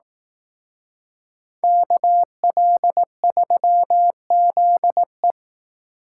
Morse code, heard as a keyed tone, about 18 wpm, 700 Hz. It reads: KL3ZE
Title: Speed Racing - USA 2x2 Call Signs 15wpm